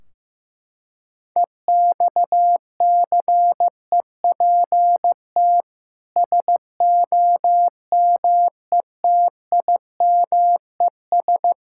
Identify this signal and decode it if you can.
Morse code, heard as a keyed tone, about 15 wpm, 700 Hz. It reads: EXCEPT SOMETIMES
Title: Sentences from Top 1600 Words 15wpm